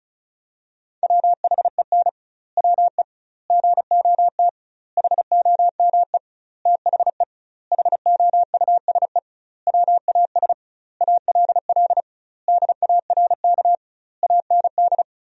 Morse code, heard as a keyed tone, about 34 wpm, 700 Hz. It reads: WHEN WE GOT HOME THE HOUSE WAS ALL DARK AND
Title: 2x Word-Spacing - Sentences from Top 500 Words - 35wpm